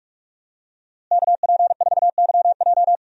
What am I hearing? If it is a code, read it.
Morse code, heard as a keyed tone, about 44 wpm, 700 Hz. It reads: KP4YJ
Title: DXCC-100 Call Signs - 5 Characters - Repeated 2x - 45wpm